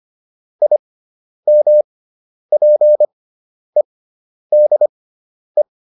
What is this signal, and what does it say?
Morse code, heard as a keyed tone, about 25 wpm, spaced as for 10 wpm, 600 Hz. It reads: IMPEDE
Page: intermediate-practice-files